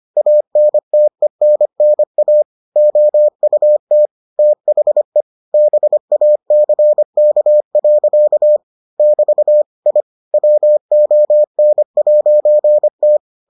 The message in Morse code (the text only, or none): ANTENNA OUT THE BACK. <BT> I WON'T